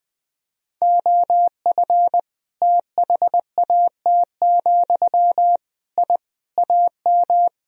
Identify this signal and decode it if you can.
Morse code, heard as a keyed tone, about 21 wpm, 700 Hz. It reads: OF THAT, I AM
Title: Sentences from Top 1000 Words - Review Part 2 - 20wpm